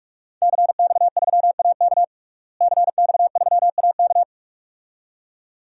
Morse code, heard as a keyed tone, about 46 wpm, 700 Hz. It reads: CX3AK CX3AK
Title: DXCC-100 Call Signs - 5 Characters - Repeated 2x - 45wpm